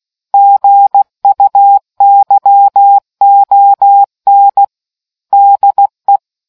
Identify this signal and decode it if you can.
Morse code, heard as a keyed tone, about 16 wpm, 800 Hz. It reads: GUYON DE